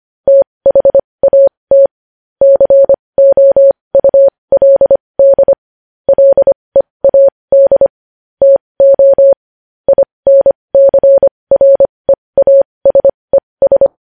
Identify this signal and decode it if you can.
Morse code, heard as a keyed tone, about 25 wpm, spaced as for 20 wpm, 550 Hz. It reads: THAT COULD LEAD TO INCREASES
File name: podcast / News headlines-1x-25x20